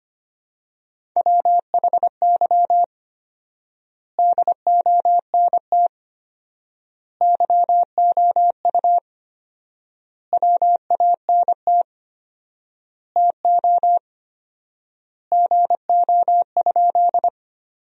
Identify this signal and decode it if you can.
Morse code, heard as a keyed tone, about 25 wpm, 700 Hz. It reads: WHY DONT YOU WANT TO GO?
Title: 4x Word-Spacing - Sentences from Top 300 Words - 25wpm